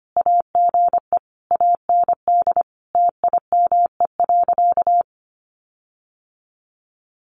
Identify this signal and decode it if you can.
Morse code, heard as a keyed tone, about 25 wpm, 700 Hz. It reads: AGE AND TIME.
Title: Sentences from Top 1300 Words 25wpm